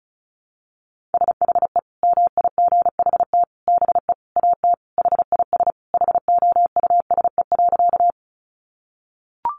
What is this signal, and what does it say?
Morse code, heard as a keyed tone, about 34 wpm, 700 Hz. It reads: SHE MIGHT BE AT HIS HOUSE.
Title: Sentences from Top 300 Words 35wpm